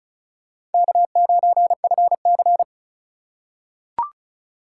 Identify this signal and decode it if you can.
Morse code, heard as a keyed tone, about 36 wpm, 700 Hz. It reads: K9FC
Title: USA Call Signs - 4 Characters - 3x Character Spacing - 35wpm